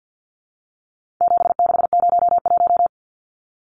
Morse code, heard as a keyed tone, about 50 wpm, 700 Hz. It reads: 7601
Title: Numbers - 4-Digit 50wpm